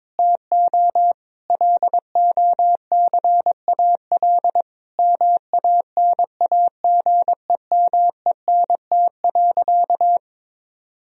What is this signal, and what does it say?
Morse code, heard as a keyed tone, about 22 wpm, 700 Hz. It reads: TO LOCAL MANAGEMENT.
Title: Sentences from Top 1000 Words - Review Part 4 - 22wpm